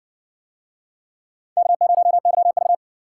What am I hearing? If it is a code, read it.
Morse code, heard as a keyed tone, about 53 wpm, 700 Hz. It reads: K0YV